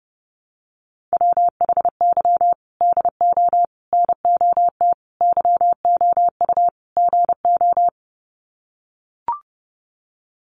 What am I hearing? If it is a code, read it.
Morse code, heard as a keyed tone, about 30 wpm, 700 Hz. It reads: WHY DO NOT YOU GO
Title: Sets of 5 Words Encore 30wpm